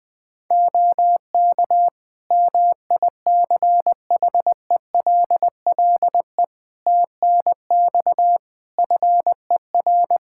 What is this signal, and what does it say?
Morse code, heard as a keyed tone, about 20 wpm, 700 Hz. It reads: OK MICHELLE TNX FER
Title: Practice QSO - Exchange 4 - Ending - 20wpm